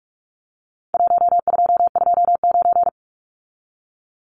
Morse code, heard as a keyed tone, about 46 wpm, 700 Hz. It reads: 1229
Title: Numbers - Contest Style - 4-Digit 45wpm